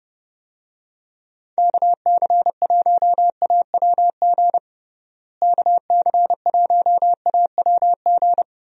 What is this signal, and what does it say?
Morse code, heard as a keyed tone, about 29 wpm, 700 Hz. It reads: KC1AWG KC1AWG
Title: USA Call Signs - 6 Characters - Repeated 3x - 30wpm